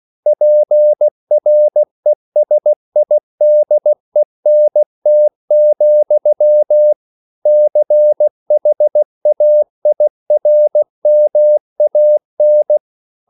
Morse code, heard as a keyed tone, about 16 wpm, 600 Hz. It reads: PRESIDENT, CHAIRMAN
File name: CW / Practice / EnglishQuotations0416